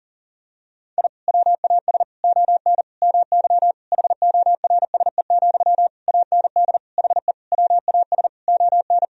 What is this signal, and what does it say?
Morse code, heard as a keyed tone, about 40 wpm, 700 Hz. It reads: I WAS ON MY HORSE, AND HE WAS ON
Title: Sentences from Top 1000 Words - Review Part 3 - 40wpm